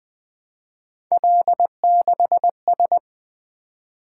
Morse code, heard as a keyed tone, about 20 wpm, 700 Hz. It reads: L6S
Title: Three Letter-Number Combinations - 20wpm